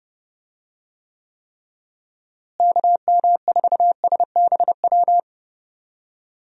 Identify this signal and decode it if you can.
Morse code, heard as a keyed tone, about 30 wpm, 700 Hz. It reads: KM4SBW
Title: USA Call Signs - 6 Characters - Repeated 3x - 30wpm